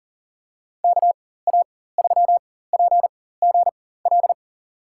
Morse code, heard as a keyed tone, about 38 wpm, spaced as for 18 wpm, 700 Hz. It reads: KA3PGL
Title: USA Call Signs - 6 Characters - 4x Character Spacing - 40wpm